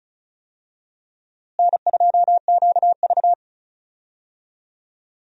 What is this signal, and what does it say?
Morse code, heard as a keyed tone, about 36 wpm, 700 Hz. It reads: N2QV
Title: USA Call Signs - 4 Characters - Repeated 2x - 35wpm